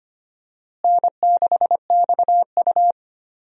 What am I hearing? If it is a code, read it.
Morse code, heard as a keyed tone, about 25 wpm, 700 Hz. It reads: N6XU
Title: USA Call Signs - 4 Characters - Repeated 3x - 25wpm